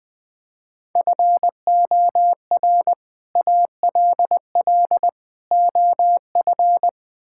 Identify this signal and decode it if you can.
Morse code, heard as a keyed tone, about 20 wpm, 700 Hz. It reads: FOR ALL OF
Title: Build A Phrase - 5-Word Phrases - 20wpm